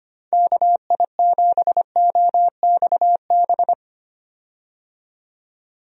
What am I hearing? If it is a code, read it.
Morse code, heard as a keyed tone, about 25 wpm, 700 Hz. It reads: KI7OXB
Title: USA Call Signs - 6 Characters - 25wpm